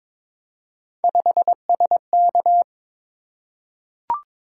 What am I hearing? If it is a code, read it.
Morse code, heard as a keyed tone, about 22 wpm, 700 Hz. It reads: 5SK
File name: practice / 22wpm / Three Letter-Number Combinations - 22wpm